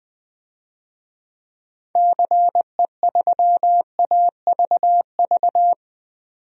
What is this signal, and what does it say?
Morse code, heard as a keyed tone, about 20 wpm, 700 Hz. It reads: CE3AVV
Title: DXCC-100 Call Signs - 6 Characters - 20wpm